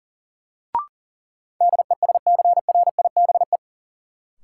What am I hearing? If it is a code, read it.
Morse code, heard as a keyed tone, about 40 wpm, 700 Hz. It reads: DESCRIBE
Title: Sentences from Top 1000 Words - New Word Drill - 40wpm